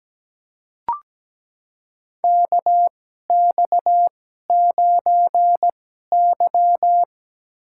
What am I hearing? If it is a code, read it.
Morse code, heard as a keyed tone, about 17 wpm, spaced as for 12 wpm, 700 Hz. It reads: KX9Y
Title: USA Call Signs - 4 Characters - 2x Character Spacing - 17wpm